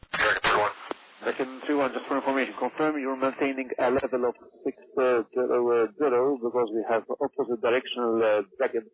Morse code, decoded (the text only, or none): none